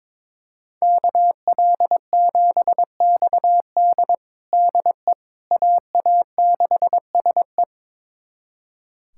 Morse code, heard as a keyed tone, about 22 wpm, 700 Hz. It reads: KL7XD DE AA6SE
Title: USA Callsign DE Callsign - 5 Characters - 22wpm